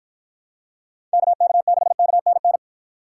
Morse code, heard as a keyed tone, about 49 wpm, 700 Hz. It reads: KK6KNN